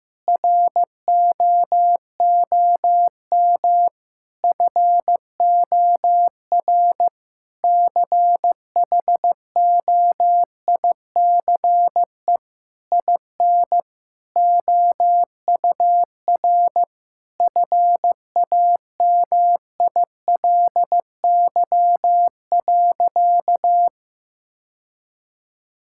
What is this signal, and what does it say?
Morse code, heard as a keyed tone, about 15 wpm, 700 Hz. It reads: ROOM FOR CHOICE IN OUR FAMILY.
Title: Sentences from Top 1000 Words - Review Part 6 - 15wpm